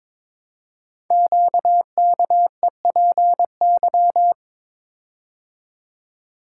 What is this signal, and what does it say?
Morse code, heard as a keyed tone, about 22 wpm, 700 Hz. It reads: QKEPY
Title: Code Groups - 5 Letters - 22wpm